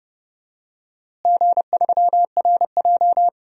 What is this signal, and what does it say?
Morse code, heard as a keyed tone, about 32 wpm, 700 Hz. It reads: G3RJ
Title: DXCC-100 Call Signs - 4 Characters - Repeated 3x - 30wpm